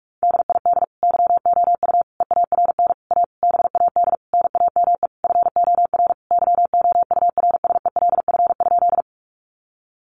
Morse code, heard as a keyed tone, about 44 wpm, 700 Hz. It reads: DID YOU EARN A BAD NAME FOR YOURSELF?
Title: Sentences from Top 1500 Words 45wpm